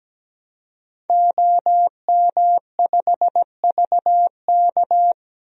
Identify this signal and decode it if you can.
Morse code, heard as a keyed tone, about 17 wpm, 700 Hz. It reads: OM5VK
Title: DXCC-100 Call Signs - 5 Characters - 17wpm